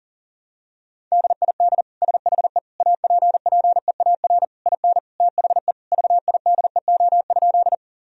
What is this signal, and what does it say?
Morse code, heard as a keyed tone, about 37 wpm, 700 Hz. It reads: DID SHE APPEAR IN THE VIDEO?
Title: Sentences from Top 1000 Words - Review Part 4 - 40wpm